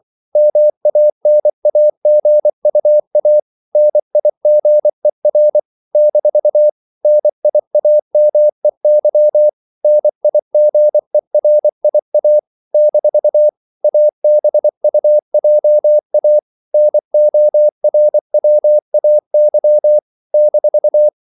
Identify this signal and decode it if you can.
Morse code, heard as a keyed tone, about 24 wpm, 600 Hz. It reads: MANAGUA NIGER - NIAMEY NIGERIA - ABUJA NORWAY -
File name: CW / Practice / NationalCapitals24